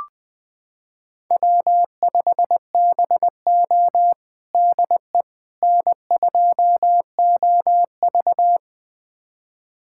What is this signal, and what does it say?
Morse code, heard as a keyed tone, about 20 wpm, 700 Hz. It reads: W5BO DE N2OV
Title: USA Callsign DE Callsign - 4 Characters - 20wpm